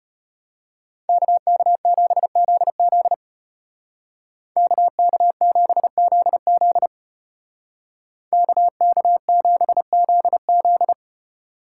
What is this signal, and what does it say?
Morse code, heard as a keyed tone, about 34 wpm, 700 Hz. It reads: KK7ZZ KK7ZZ KK7ZZ